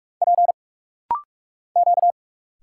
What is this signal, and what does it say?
Morse code, heard as a keyed tone, about 46 wpm, 690 Hz. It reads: P Q
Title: Single Letters - Rapid-Fire 45wpm